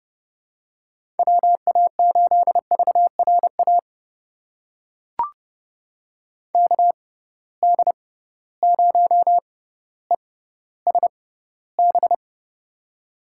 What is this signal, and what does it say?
Morse code, heard as a keyed tone, about 32 wpm, spaced as for 16 wpm, 700 Hz. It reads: WA8VRA KD0ESB
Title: USA Call Signs - 6 Characters - 6x Character Spacing - 30wpm